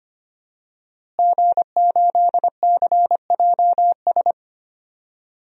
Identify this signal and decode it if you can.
Morse code, heard as a keyed tone, about 25 wpm, 700 Hz. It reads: G8CJS
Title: DXCC-100 Call Signs - 5 Characters - Repeated 3x - 25wpm